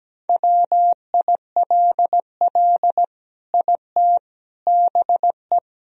Morse code, heard as a keyed tone, about 17 wpm, 700 Hz. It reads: WILL IT BE
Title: Sentences from Top 1000 Words - Review Part 2 - 17wpm